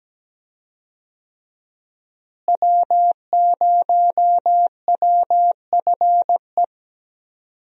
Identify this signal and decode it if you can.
Morse code, heard as a keyed tone, about 17 wpm, 700 Hz. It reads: W0WFE
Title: USA Call Signs - 5 Characters - Repeated 3x - 17wpm